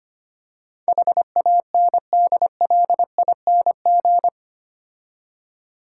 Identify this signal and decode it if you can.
Morse code, heard as a keyed tone, about 25 wpm, 700 Hz. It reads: HANDLING
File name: 8 Letter Words - First 3 Letters Given - 25wpm